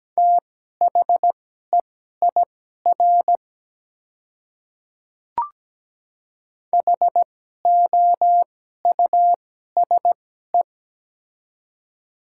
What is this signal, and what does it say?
Morse code, heard as a keyed tone, about 17 wpm, spaced as for 12 wpm, 690 Hz. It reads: THEIR HOUSE
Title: 2x Character Spacing - Top 300 Words - 17wpm